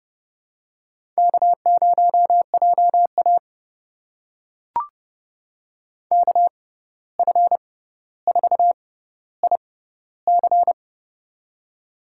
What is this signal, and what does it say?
Morse code, heard as a keyed tone, about 30 wpm, spaced as for 14 wpm, 700 Hz. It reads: K0JA KF4IC
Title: USA Call Signs - 6x Character Spacing - 30wpm